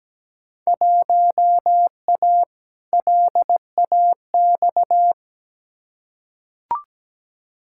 Morse code, heard as a keyed tone, about 17 wpm, 700 Hz. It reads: 1A LAX
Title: Field Day Exchanges Encore - 17wpm